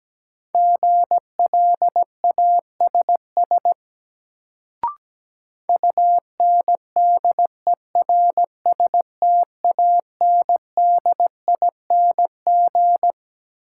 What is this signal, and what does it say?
Morse code, heard as a keyed tone, about 17 wpm, 700 Hz. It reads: GLASS UNDERSTANDING
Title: Sentences from Top 1000 Words - Review Part 3 - New Word Drill - 17wpm